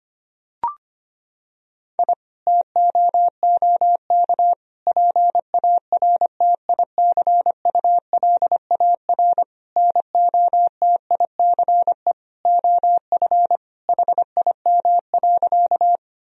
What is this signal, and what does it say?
Morse code, heard as a keyed tone, about 25 wpm, 700 Hz. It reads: I TOOK PARTICULAR NOTICE OF HIM.
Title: Sentences from Top 1000 Words - Review Part 4 - 25wpm